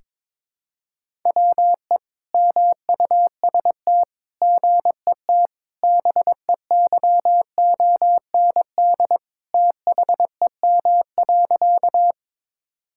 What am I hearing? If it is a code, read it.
Morse code, heard as a keyed tone, about 21 wpm, 700 Hz. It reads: WE MUST GET BEYOND THEM.
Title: Sentences from Top 1000 Words - Review Part 6 - 22wpm